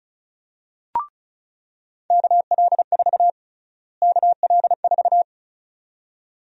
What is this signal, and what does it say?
Morse code, heard as a keyed tone, about 36 wpm, 700 Hz. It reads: KL4 KL4
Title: USA Call Sign Prefixes - Repeated 2x - 35wpm